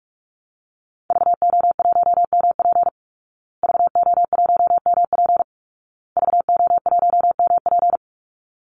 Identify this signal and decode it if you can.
Morse code, heard as a keyed tone, about 44 wpm, 700 Hz. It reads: VO1MP VO1MP VO1MP
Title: Call Signs Repeated 2x - 45wpm